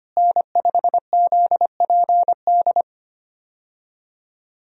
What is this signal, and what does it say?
Morse code, heard as a keyed tone, about 25 wpm, 700 Hz. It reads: N5ZPD
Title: USA Call Signs - 5 Characters - Repeated 2x - 25wpm